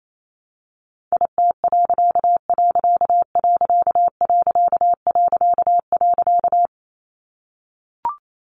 Morse code, heard as a keyed tone, about 28 wpm, 700 Hz. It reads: IT......
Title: A Sound Of Thunder 28wpm